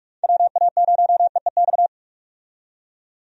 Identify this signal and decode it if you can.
Morse code, heard as a keyed tone, about 46 wpm, 700 Hz. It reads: WA0EEX
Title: USA Call Signs - 6 Characters - 45wpm